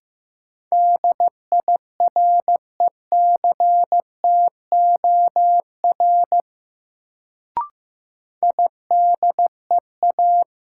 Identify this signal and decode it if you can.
Morse code, heard as a keyed tone, about 15 wpm, 700 Hz. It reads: DIRECTOR IDEA
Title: Sets of 3 Words - New Word Drill - 15wpm